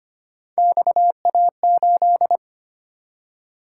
Morse code, heard as a keyed tone, about 29 wpm, 700 Hz. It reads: XA8
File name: Speed-Racing - Three Letter-Number Combinations - 25wpm